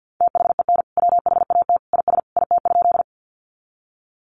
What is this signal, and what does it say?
Morse code, heard as a keyed tone, about 50 wpm, 700 Hz. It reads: THEN WHAT IS IT?
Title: Sentences from Top 100 Words 50wpm